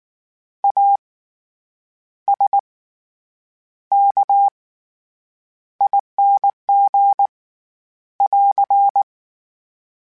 Morse code, heard as a keyed tone, about 19 wpm, spaced as for 8 wpm, 800 Hz. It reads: ASKING <AR>
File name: Special INGs / Special INGs-mp3s / SlowToFaster ing @18